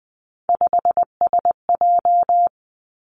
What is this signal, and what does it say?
Morse code, heard as a keyed tone, about 20 wpm, 700 Hz. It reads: 5SJ
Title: Three Challenging Letter-Number Combinations 20wpm